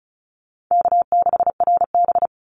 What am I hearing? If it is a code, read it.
Morse code, heard as a keyed tone, about 34 wpm, 700 Hz. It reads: K6RB
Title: Easy Call Signs Repeated 3x - 35wpm